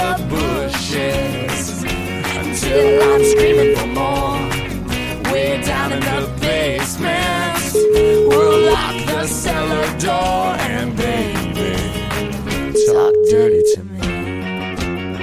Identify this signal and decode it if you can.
Ringback tone (Europe, 425 Hz), with speech.